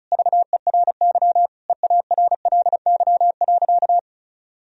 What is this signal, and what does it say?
Morse code, heard as a keyed tone, about 36 wpm, 700 Hz. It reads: VERY EARLY.
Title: Sentences from Top 1000 Words - Review Part 4 - 35wpm